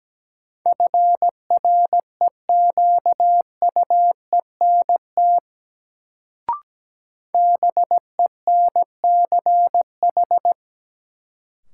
Morse code, heard as a keyed tone, about 17 wpm, 700 Hz. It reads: FREQUENT BENCH
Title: Sentences from Top 1400 Words - New Word Drill Alt - 17wpm